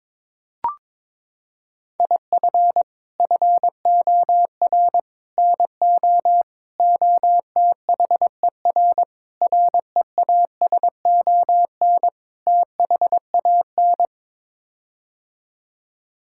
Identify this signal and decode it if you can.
Morse code, heard as a keyed tone, about 22 wpm, 700 Hz. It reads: IF FOR NO OTHER REASON THAN
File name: practice / 22wpm / Sets of 6 Words - 22wpm